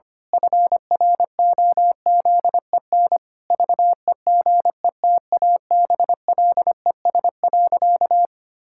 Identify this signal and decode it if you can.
Morse code, heard as a keyed tone, about 25 wpm, 700 Hz. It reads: FROZEN VEGETABLES.
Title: Pangram Challenge v2 - 25wpm